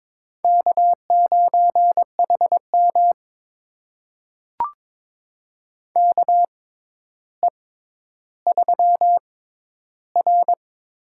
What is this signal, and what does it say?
Morse code, heard as a keyed tone, about 22 wpm, spaced as for 11 wpm, 700 Hz. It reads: K9HM KE3R
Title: USA Call Signs - 4 Characters - 6x Character Spacing - 22wpm